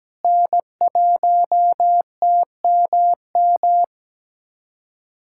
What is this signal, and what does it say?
Morse code, heard as a keyed tone, about 17 wpm, 700 Hz. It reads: N1TMM
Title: USA Callsign DE Callsign - 5 Characters - 17wpm